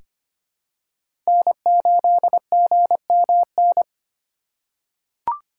Morse code, heard as a keyed tone, about 25 wpm, 700 Hz. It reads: N8GMN
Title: USA Call Signs - 5 Characters - 2x Character Spacing - 25wpm